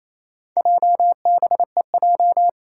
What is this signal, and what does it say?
Morse code, heard as a keyed tone, about 28 wpm, 700 Hz. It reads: JBEJ
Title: Four Challenging Letter-Number Combinations 28wpm